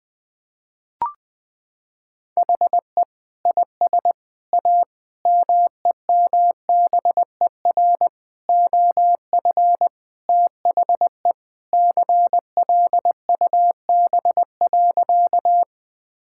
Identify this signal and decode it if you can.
Morse code, heard as a keyed tone, about 20 wpm, 700 Hz. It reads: HE IS A MEMBER OF THE CLUB.
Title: Sentences with 3-Word Phrases - 20wpm